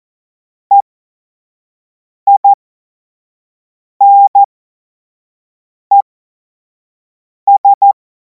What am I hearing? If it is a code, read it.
Morse code, heard as a keyed tone, about 14 wpm, spaced as for 5 wpm, 800 Hz. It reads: EINES